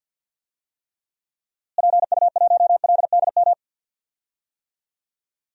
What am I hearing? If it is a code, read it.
Morse code, heard as a keyed tone, about 48 wpm, 700 Hz. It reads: PU1LDM
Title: DXCC-100 Call Signs - 6 Characters - Repeated 3x - 50wpm